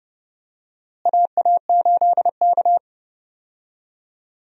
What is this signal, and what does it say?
Morse code, heard as a keyed tone, about 32 wpm, 700 Hz. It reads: AA8K
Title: USA Call Signs - 4 Characters - Repeated 3x - 30wpm